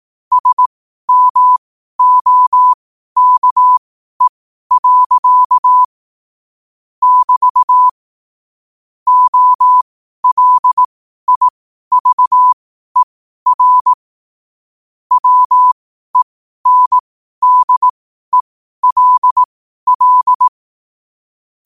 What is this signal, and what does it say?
Morse code, heard as a keyed tone, about 18 wpm, spaced as for 12 wpm, 1000 Hz. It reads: SMOKE. <BT> OLIVER WENDELL
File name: qotdfiles / 12wpm